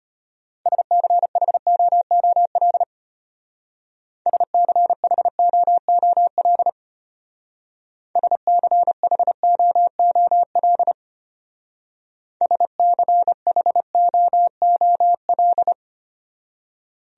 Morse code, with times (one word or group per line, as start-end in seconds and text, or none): 0.7-2.8 SCHOOL
4.3-6.7 SCHOOL
8.2-10.9 SCHOOL
12.4-15.7 SCHOOL